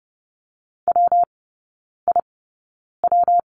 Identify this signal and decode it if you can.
Morse code, heard as a keyed tone, about 30 wpm, spaced as for 9 wpm, 700 Hz. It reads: WIW
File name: Lesson 19e - Letter w - Three Characters 30wpm